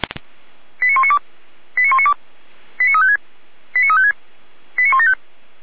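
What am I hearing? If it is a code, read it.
Five-tone selective call (ZVEI), heard as quick runs of five tones, tones of about 80 ms. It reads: 89192 89192 89356 89356 89175
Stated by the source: ZVEI five-tone selective call 89340 received on SDR.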